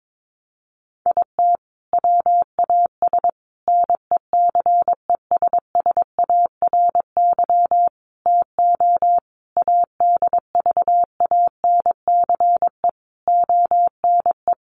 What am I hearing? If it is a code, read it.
Morse code, heard as a keyed tone, about 22 wpm, 700 Hz. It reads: IT WAS NECESSARY TO ADVANCE ONE
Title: Sentences from Top 1400 Words 22wpm